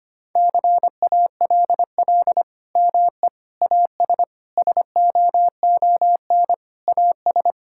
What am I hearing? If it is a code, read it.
Morse code, heard as a keyed tone, about 25 wpm, 700 Hz. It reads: CALL ME AS SOON AS
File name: practice / 25wpm / Sentences with 3-Word Phrases - 25wpm